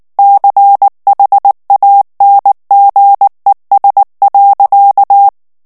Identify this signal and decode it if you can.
Morse code, heard as a keyed tone, about 19 wpm, 800 Hz. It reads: CHANGES.
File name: ANGLAIS texte clair 3 vitesse 18 mots minute